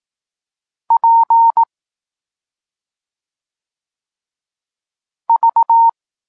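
Morse code, heard as a keyed tone, about 18 wpm, 920 Hz. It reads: P V
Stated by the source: Polish Morse lesson teaching the letter C, from SP9MOA Niepołomicki Klub Krótkofalowców.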